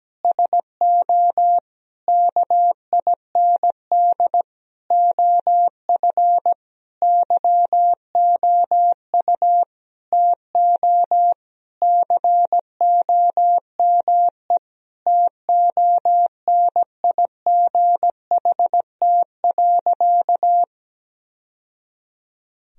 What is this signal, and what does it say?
Morse code, heard as a keyed tone, about 17 wpm, 700 Hz. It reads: SO KIND OF YOU TO COME TONIGHT.
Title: Sentences from Top 1000 Words - Review Part 3 - 17wpm